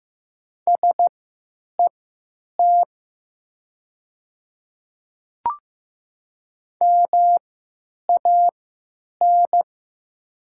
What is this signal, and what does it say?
Morse code, heard as a keyed tone, about 15 wpm, spaced as for 8 wpm, 700 Hz. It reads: SET MAN
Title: 3x Character Spacing - Top 200 Words - 15wpm